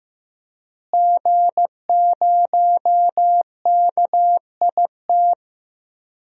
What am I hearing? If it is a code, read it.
Morse code, heard as a keyed tone, about 15 wpm, 700 Hz. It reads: G0KIT
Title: DXCC-100 Call Signs - 5 Characters - Repeated 2x - 15wpm